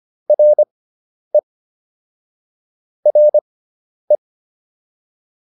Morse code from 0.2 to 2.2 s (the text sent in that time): RE